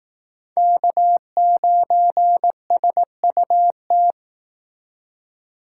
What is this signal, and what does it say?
Morse code, heard as a keyed tone, about 18 wpm, 700 Hz. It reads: K9SUT
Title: Speed Racing - USA 1x3 Call Signs 15wpm